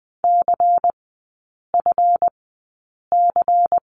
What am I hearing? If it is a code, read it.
Morse code, heard as a keyed tone, about 20 wpm, spaced as for 8 wpm, 700 Hz. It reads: CFC
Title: 178 Lesson 29e - Slash - Three Characters 20wpm